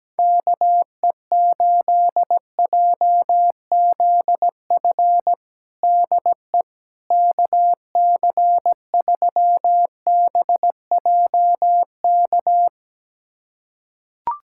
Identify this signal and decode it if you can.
Morse code, heard as a keyed tone, about 17 wpm, 700 Hz. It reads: KE8JZF DE KC3BJK
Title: USA Callsign DE Callsign - 6 Characters - 17wpm